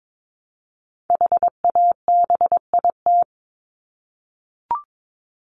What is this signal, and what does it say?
Morse code, heard as a keyed tone, about 22 wpm, 700 Hz. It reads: HABIT
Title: Top 1751-2000 Words 22wpm